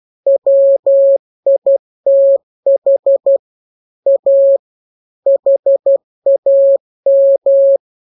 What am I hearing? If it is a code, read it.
Morse code, heard as a keyed tone, about 12 wpm, 550 Hz. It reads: WITH A HAM
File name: jokes / joke 18 12 0000